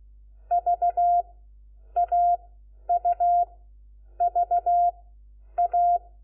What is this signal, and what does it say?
Morse code, heard as a keyed tone, about 16 wpm, 700 Hz. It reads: V A U V A
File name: Lecciones / CLASE Nº4